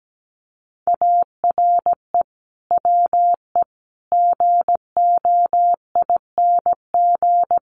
Morse code, heard as a keyed tone, about 17 wpm, 700 Hz. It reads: ARE WE GOING
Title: Sets of 4 Words 17wpm